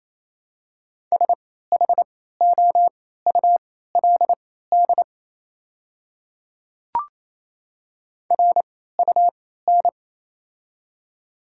Morse code, heard as a keyed tone, about 28 wpm, spaced as for 16 wpm, 700 Hz. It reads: SHOULD RUN
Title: 3x Character Spacing - Top 300 Words - 28wpm